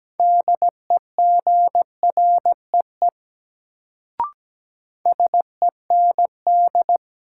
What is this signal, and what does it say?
Morse code, heard as a keyed tone, about 17 wpm, 700 Hz. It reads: DEGREE SEND
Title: Sentences from Top 1500 Words - New Word Drill - 17wpm